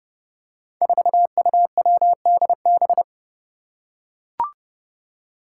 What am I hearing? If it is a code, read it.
Morse code, heard as a keyed tone, about 30 wpm, 700 Hz. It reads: 4UWDB
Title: Five Challenging Letter-Number Combinations 30wpm